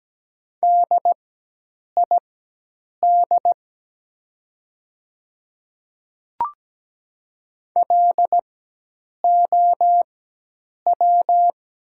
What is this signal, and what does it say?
Morse code, heard as a keyed tone, about 17 wpm, spaced as for 8 wpm, 700 Hz. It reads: DID LOW
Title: 4x Character Spacing - Top 200 Words - 17wpm